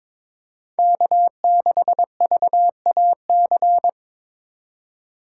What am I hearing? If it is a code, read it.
Morse code, heard as a keyed tone, about 22 wpm, 700 Hz. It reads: K6VAC
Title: USA Call Signs - 5 Characters - 3x Character Spacing - 22wpm